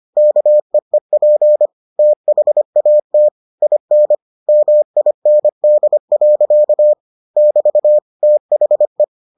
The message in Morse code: KEEP THAT IN MIND. <BT> THE

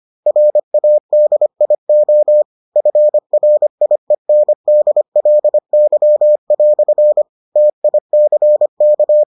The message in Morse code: RADIO FRIENDLY" TICK

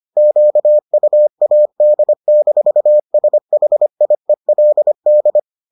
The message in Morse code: QUAD-SHIELD

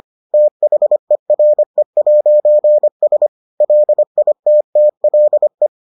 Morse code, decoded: THERE'S LITTLE

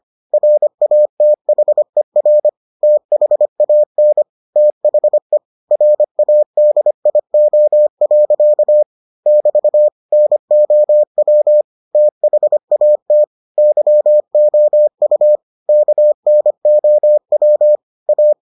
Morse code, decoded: RATHER THAN THE RADIO. <BT> NOW THAT YOU KNOW A

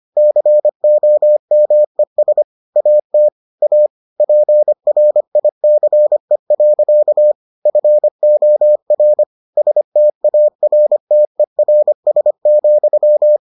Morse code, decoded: COMES AT A PRICE. FOR STARTERS,

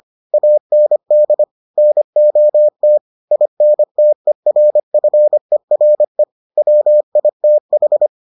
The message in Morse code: AND NOT INTERFERE WITH

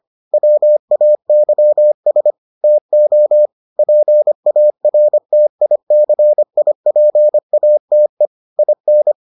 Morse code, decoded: WAYS TO PARTICIPATE IN

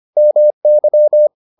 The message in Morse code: MY

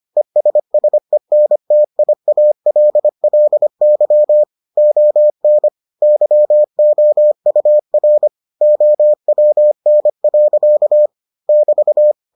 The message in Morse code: ESSENTIALLY ON YOUR OWN. <BT>

